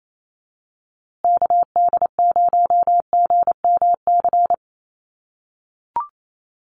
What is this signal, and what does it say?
Morse code, heard as a keyed tone, about 28 wpm, 700 Hz. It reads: KD0GMC
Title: USA Call Signs Repeated 3x 28wpm